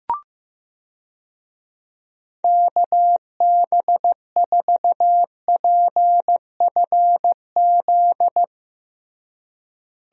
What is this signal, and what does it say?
Morse code, heard as a keyed tone, about 15 wpm, 700 Hz. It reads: KB4PFZ
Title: USA Call Signs - 6 Characters - Repeated 2x - 15wpm